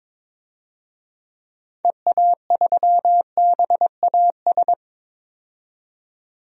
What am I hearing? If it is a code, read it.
Morse code, heard as a keyed tone, about 22 wpm, 700 Hz. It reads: EA3BAS
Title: DXCC-100 Call Signs - 6 Characters - 22wpm